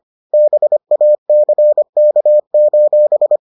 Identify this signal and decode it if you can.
Morse code, heard as a keyed tone, about 25 wpm, 600 Hz. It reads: BACK: